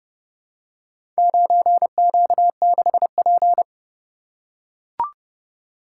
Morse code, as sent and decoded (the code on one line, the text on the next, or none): ----. --.- -.... .--.
9Q6P